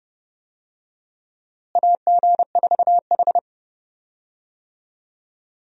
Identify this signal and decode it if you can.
Morse code, heard as a keyed tone, about 30 wpm, 700 Hz. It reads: AG4H
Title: USA Call Signs - 4 Characters - Repeated 2x - 30wpm